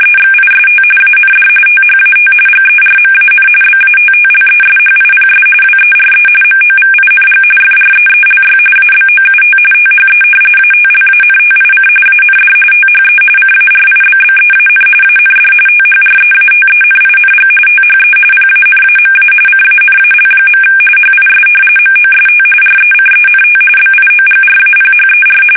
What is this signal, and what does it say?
RTTY, about 45.45 baud. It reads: EXIST AND THAT GERMAN BATTLESHIPS WERE MOVING FREELY ABOUT THE ATLANTIC AND THE CARIBBEAN AND THREATENING THE PANAMA CANAL. HIS MOST RECENT BIOGRAPHER
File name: fsk850